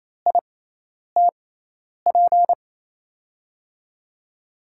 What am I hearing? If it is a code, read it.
Morse code, heard as a keyed tone, about 28 wpm, spaced as for 10 wpm, 700 Hz. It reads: ITP
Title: USA Call Signs - 6x Character Spacing - 28wpm